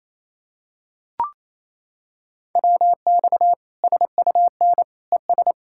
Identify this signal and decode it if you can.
Morse code, heard as a keyed tone, about 28 wpm, 700 Hz. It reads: WX SUN ES
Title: Practice QSO - Exchange 2 - Info - 28wpm